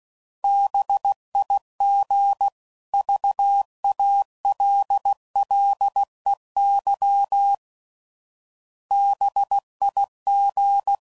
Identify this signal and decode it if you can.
Morse code, heard as a keyed tone, about 16 wpm, 800 Hz. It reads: BIG VALLEY BIG
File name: CW Code Groups @15WPM